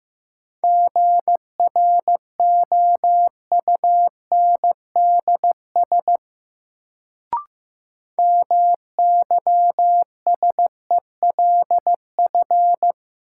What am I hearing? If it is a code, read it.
Morse code, heard as a keyed tone, about 15 wpm, 700 Hz. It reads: GROUNDS MYSELF
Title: Sentences from Top 1300 Words - New Word Drill - 15wpm